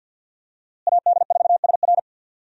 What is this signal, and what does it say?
Morse code, heard as a keyed tone, about 46 wpm, 700 Hz. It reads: AD4SR
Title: USA Call Signs - 5 Characters - Repeated 2x - 50wpm